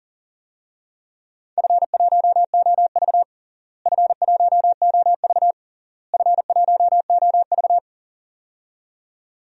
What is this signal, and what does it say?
Morse code, heard as a keyed tone, about 40 wpm, 700 Hz. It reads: F1OV F1OV F1OV
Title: DXCC-100 Call Signs - 4 Characters - Repeated 3x - 40wpm